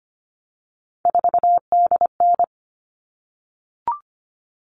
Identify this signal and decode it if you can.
Morse code, heard as a keyed tone, about 25 wpm, 700 Hz. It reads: 4DN
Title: All Cut Numbers - 3-Digit 25wpm